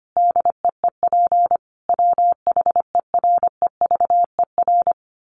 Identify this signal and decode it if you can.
Morse code, heard as a keyed tone, about 25 wpm, 700 Hz. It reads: DEEP WHEREVER
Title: A Sound Of Thunder 25wpm